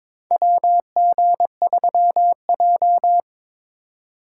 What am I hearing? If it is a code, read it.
Morse code, heard as a keyed tone, about 22 wpm, 690 Hz. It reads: WG3J
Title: USA Call Signs - 4 Characters - 3x Character Spacing - 22wpm